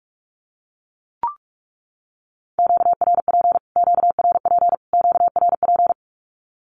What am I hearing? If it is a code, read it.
Morse code, heard as a keyed tone, about 44 wpm, 700 Hz. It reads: QRP QRP QRP
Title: Q-Codes Repeated 3x 45wpm